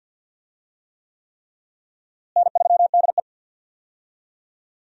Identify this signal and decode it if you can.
Morse code, heard as a keyed tone, about 48 wpm, 700 Hz. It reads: N3DE
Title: USA Call Signs - 4 Characters - Repeated 3x - 50wpm